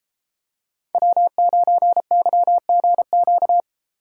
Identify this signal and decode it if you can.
Morse code, heard as a keyed tone, about 30 wpm, 700 Hz. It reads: W9YGQ